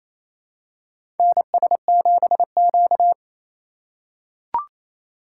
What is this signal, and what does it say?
Morse code, heard as a keyed tone, about 28 wpm, 700 Hz. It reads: NS7Q
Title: USA Call Signs - 4 Characters - 4x Character Spacing - 28wpm